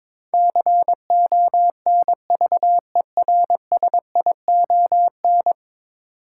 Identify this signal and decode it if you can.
Morse code, heard as a keyed tone, about 22 wpm, 700 Hz. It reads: CONVERSION
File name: practice / 22wpm / 10 Letter Words - First 3 Letters Given - 22wpm